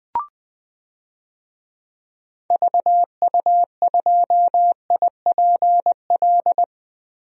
Morse code, heard as a keyed tone, about 21 wpm, 700 Hz. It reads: VU2IPL
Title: DXCC-100 Call Signs - 6 Characters - Repeated 3x - 20wpm